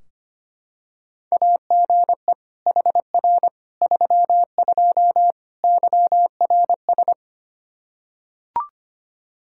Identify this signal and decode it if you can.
Morse code, heard as a keyed tone, about 25 wpm, 700 Hz. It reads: AGE HR 32 YRS
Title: Practice QSO - Exchange 3 - More Info - 25wpm